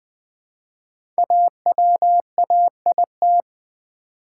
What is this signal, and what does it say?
Morse code, heard as a keyed tone, about 20 wpm, 700 Hz. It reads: AWAIT
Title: Build A Word - 1 Letter At A Time - 5-Letter Words - 20wpm